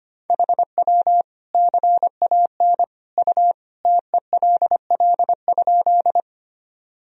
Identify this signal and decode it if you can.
Morse code, heard as a keyed tone, about 25 wpm, 700 Hz. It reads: HW CAN U TELL?
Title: Sentences from Top 200 Words and Abbreviations 25wpm